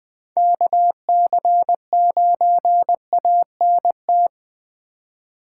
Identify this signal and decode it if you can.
Morse code, heard as a keyed tone, about 20 wpm, 700 Hz. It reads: KC9ANT
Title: USA Call Signs - 6 Characters - 2x Character Spacing - 20wpm